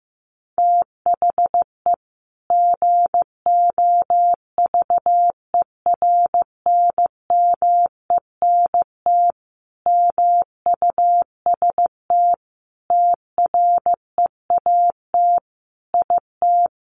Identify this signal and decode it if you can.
Morse code, heard as a keyed tone, about 15 wpm, 700 Hz. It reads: THE GOVERNMENT MUST TREAT IT
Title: Sentences from Top 1300 Words 15wpm